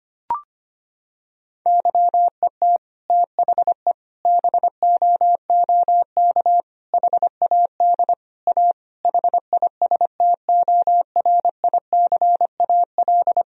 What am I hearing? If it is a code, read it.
Morse code, heard as a keyed tone, about 25 wpm, 700 Hz. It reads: YET THE BOOK HAD A HISTORICAL
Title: Sentences from Top 1000 Words - Review Part 4 - 25wpm